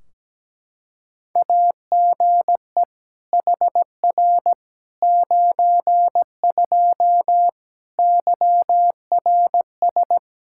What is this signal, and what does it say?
Morse code, heard as a keyed tone, about 17 wpm, 700 Hz. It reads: AGE HR 92 YRS
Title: K1USN SST QSO Practice 17wpm